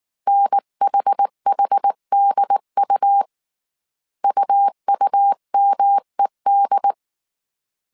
Morse code, heard as a keyed tone, about 19 wpm, 800 Hz. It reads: NHHDU UUMED